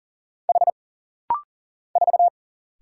Morse code, heard as a keyed tone, about 40 wpm, 700 Hz. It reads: H 4
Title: Single Letter-Number Alt - Rapid-Fire 40wpm